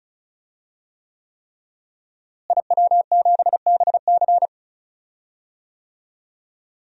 Morse code, heard as a keyed tone, about 36 wpm, 700 Hz. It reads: IW7BC
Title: DXCC-100 Call Signs - 5 Characters - Repeated 3x - 35wpm